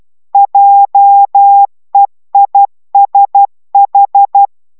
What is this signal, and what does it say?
Morse code, heard as a keyed tone, about 12 wpm, 800 Hz. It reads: JEISH